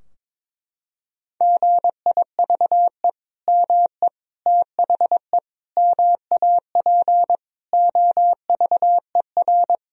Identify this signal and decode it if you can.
Morse code, heard as a keyed tone, about 22 wpm, 700 Hz. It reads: GIVE ME THE MAP OVER
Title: Sentences from Top 1000 Words - Review Part 2 - 22wpm